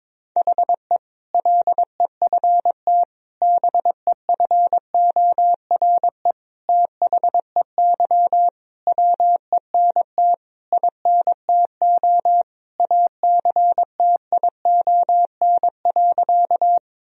Morse code, heard as a keyed tone, about 22 wpm, 700 Hz. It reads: HE LEFT BEFORE THEY WENT INTO ACTION.
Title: Sentences from Top 1000 Words - Review Part 4 - 22wpm